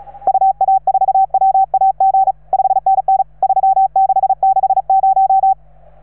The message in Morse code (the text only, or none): UA4WAG 5NN 3660